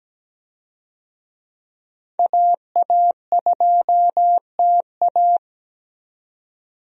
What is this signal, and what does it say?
Morse code, heard as a keyed tone, about 17 wpm, 700 Hz. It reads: AA2TA
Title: USA Call Signs - 5 Characters - 17wpm